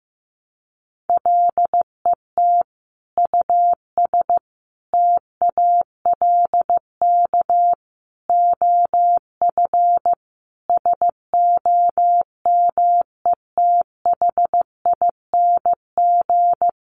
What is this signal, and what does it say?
Morse code, heard as a keyed tone, about 15 wpm, 700 Hz. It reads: LET US TALK OF SOMETHING
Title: Sentences from Top 800 Words 15wpm